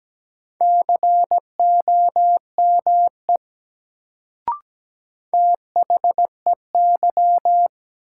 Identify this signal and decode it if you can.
Morse code, heard as a keyed tone, about 17 wpm, 700 Hz. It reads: COME THEY
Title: Sentences from Top 100 Words - New Word Drill - 17wpm